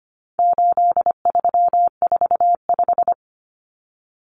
Morse code, heard as a keyed tone, about 25 wpm, 700 Hz. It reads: 8345
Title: Common Cut Numbers - 4-Digit 25wpm